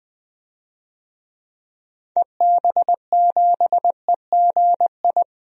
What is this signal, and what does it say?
Morse code, heard as a keyed tone, about 19 wpm, 700 Hz. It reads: EB7EGI
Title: DXCC-100 Call Signs - 6 Characters - Repeated 3x - 20wpm